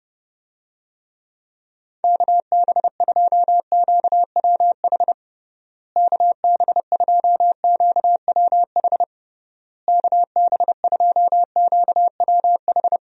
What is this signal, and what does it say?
Morse code, heard as a keyed tone, about 30 wpm, 700 Hz. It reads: KB2QWH KB2QWH KB2QWH
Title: USA Call Signs - 6 Characters - Repeated 3x - 30wpm